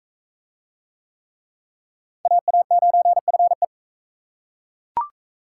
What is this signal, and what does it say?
Morse code, heard as a keyed tone, about 41 wpm, 700 Hz. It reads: AA9FE